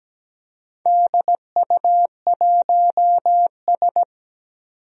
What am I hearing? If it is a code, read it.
Morse code, heard as a keyed tone, about 17 wpm, 700 Hz. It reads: DU1S
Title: DXCC-100 Call Signs - 4 Characters - 17wpm